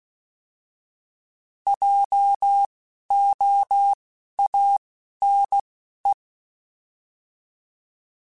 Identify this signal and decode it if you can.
Morse code, heard as a keyed tone, about 16 wpm, 800 Hz. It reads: J O A N E
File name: Alphabetical / Alphabetical mp3s / Js @10